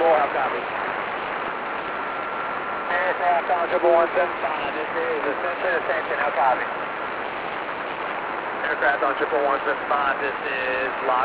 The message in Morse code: none

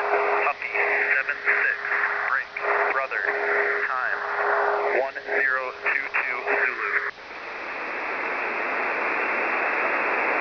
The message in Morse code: none

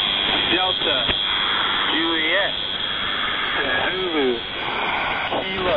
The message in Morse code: none

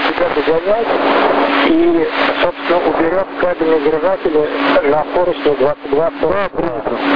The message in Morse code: none

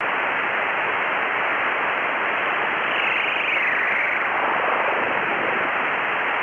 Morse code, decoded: none

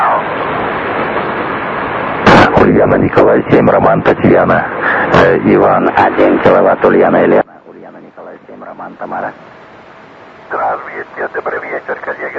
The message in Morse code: none

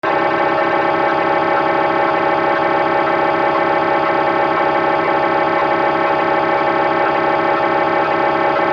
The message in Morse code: none

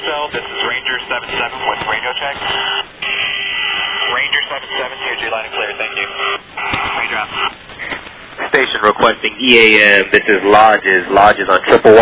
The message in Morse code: none

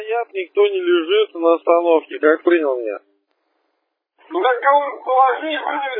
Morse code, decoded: none

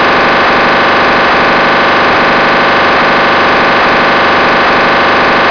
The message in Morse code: none

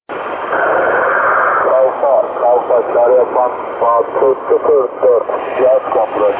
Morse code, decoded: none

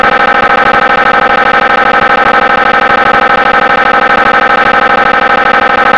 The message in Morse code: none

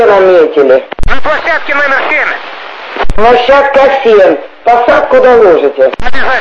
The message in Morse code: none

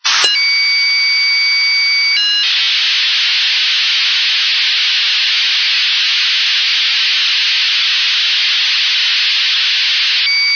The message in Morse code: none

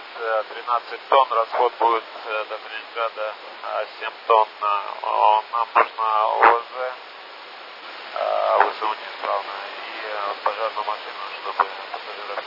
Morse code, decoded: none

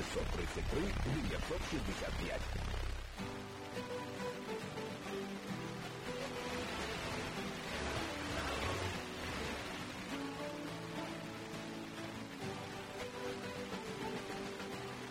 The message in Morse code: none

